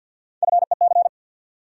Morse code, heard as a keyed tone, about 48 wpm, 700 Hz. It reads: FEC
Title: USA Call Sign Postfixes - 3 Letters 50wpm